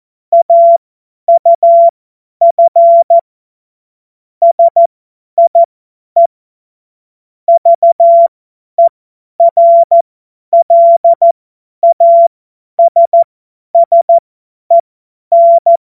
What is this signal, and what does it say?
Morse code, heard as a keyed tone, about 14 wpm, spaced as for 10 wpm, 680 Hz. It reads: AUF SIE VERLASSEN
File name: Morsekurs / Witze / Witz 24. Januar 2023 50 F70